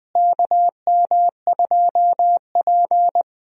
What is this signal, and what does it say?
Morse code, heard as a keyed tone, about 21 wpm, 700 Hz. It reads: KM2P
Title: USA Call Signs - 4 Characters - Repeated 3x - 20wpm